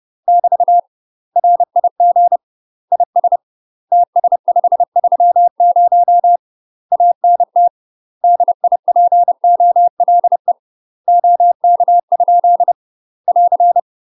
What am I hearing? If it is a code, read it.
Morse code, heard as a keyed tone, about 30 wpm, 700 Hz. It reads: <BT> RIG IS TS530 ANT DIPOLE OK? <AR>
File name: UK MASTER QSOs TRAINING / 30WPM / UK MASTER QSOs 0031